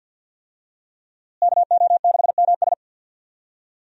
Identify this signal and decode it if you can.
Morse code, heard as a keyed tone, about 49 wpm, 700 Hz. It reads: KO6MS
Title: USA Call Signs - 5 Characters - Repeated 2x - 50wpm